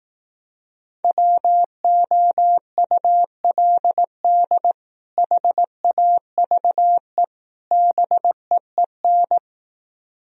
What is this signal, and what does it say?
Morse code, heard as a keyed tone, about 18 wpm, 700 Hz. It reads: WOULD HAVE BEEN
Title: Speed-Racing - Sets of 3 Words 15wpm